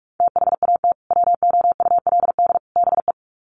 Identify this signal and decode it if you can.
Morse code, heard as a keyed tone, about 44 wpm, 700 Hz. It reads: THAT WOULD BE
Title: Sets of 3 Words 45wpm